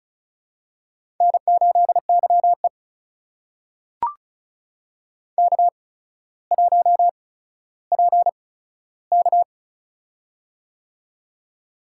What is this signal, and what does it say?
Morse code, heard as a keyed tone, about 36 wpm, spaced as for 15 wpm, 700 Hz. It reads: N8YE K1PK
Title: USA Call Signs - 4 Characters - 8x Character Spacing - 35wpm